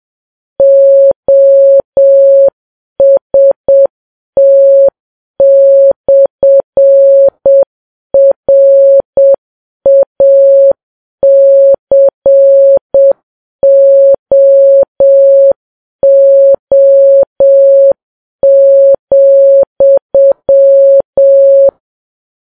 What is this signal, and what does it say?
Morse code, heard as a keyed tone, about 7 wpm, 550 Hz. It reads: OST/RACOO,
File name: Files Web tls / Morse on9fv / morse7kps